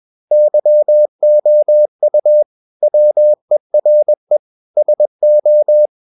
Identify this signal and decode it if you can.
Morse code, heard as a keyed tone, about 21 wpm, 600 Hz. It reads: YOU WERE SO
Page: CW med Gnister